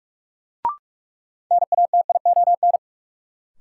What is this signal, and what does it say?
Morse code, heard as a keyed tone, about 44 wpm, 700 Hz. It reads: NATION
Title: Sets of 3 Words - New Word Drill - 45wpm